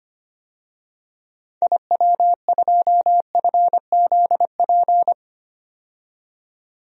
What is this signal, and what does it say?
Morse code, heard as a keyed tone, about 25 wpm, 700 Hz. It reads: IW2FZP
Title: DXCC-100 Call Signs - 6 Characters - 25wpm